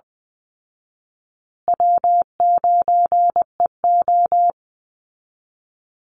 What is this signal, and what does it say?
Morse code, heard as a keyed tone, about 20 wpm, 700 Hz. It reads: W9EO
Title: Easy Call Signs Repeated 2x - 20wpm